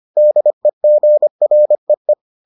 Morse code, heard as a keyed tone, about 25 wpm, 600 Hz. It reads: DEGREE